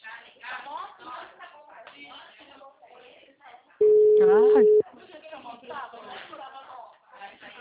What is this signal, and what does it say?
Ringback tone (Europe, 425 Hz), with speech.